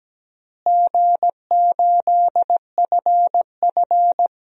Morse code, heard as a keyed tone, about 17 wpm, 700 Hz. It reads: G8FF
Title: DXCC-100 Call Signs - 4 Characters - Repeated 2x - 17wpm